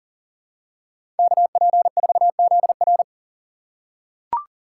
Morse code, heard as a keyed tone, about 40 wpm, 700 Hz. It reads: KP4ZR
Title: USA Call Signs - 5 Characters - 2x Character Spacing - 40wpm